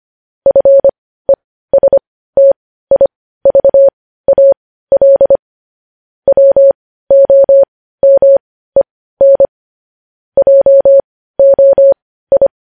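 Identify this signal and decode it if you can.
Morse code, heard as a keyed tone, about 25 wpm, spaced as for 15 wpm, 550 Hz. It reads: FESTIVAL WOMEN JOI
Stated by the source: Morse news headlines practice at 15.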